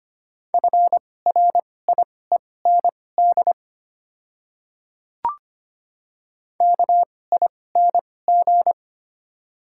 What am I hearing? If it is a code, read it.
Morse code, heard as a keyed tone, about 25 wpm, spaced as for 18 wpm, 700 Hz. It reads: FRIEND KING
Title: 2x Character Spacing - Top 500 Words - 25wpm